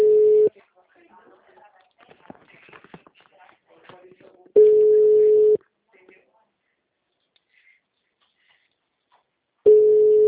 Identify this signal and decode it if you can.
Ringback tone (Europe, 425 Hz), with speech.